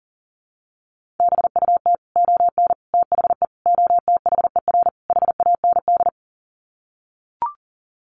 Morse code, heard as a keyed tone, about 40 wpm, 700 Hz. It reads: BUT ON THE OTHER HAND
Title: Sets of 5 Words 40wpm